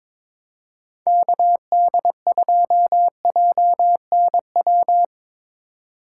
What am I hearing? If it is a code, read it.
Morse code, heard as a keyed tone, about 24 wpm, 700 Hz. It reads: KD2JNW
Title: Speed Racing - USA Call Signs 22wpm